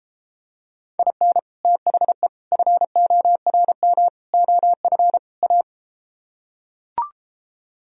Morse code, heard as a keyed tone, about 33 wpm, 700 Hz. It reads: IN THE FORM OF A